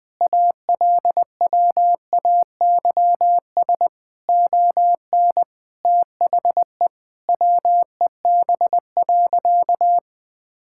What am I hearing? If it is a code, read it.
Morse code, heard as a keyed tone, about 20 wpm, 700 Hz. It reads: ALWAYS ON THE WEB.
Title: All-Age Appropriate Jokes - 20wpm